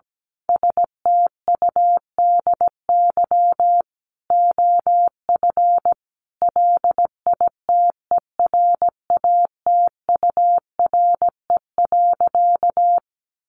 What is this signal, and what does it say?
Morse code, heard as a keyed tone, about 17 wpm, 700 Hz. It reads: STUDY OF LITERATURE.
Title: Sentences from Top 900 Words 17wpm